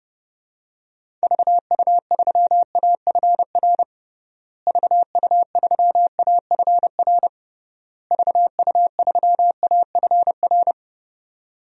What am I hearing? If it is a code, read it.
Morse code, heard as a keyed tone, about 30 wpm, 700 Hz. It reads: VU3AFR VU3AFR VU3AFR
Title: DXCC-100 Call Signs - 6 Characters - Repeated 3x - 30wpm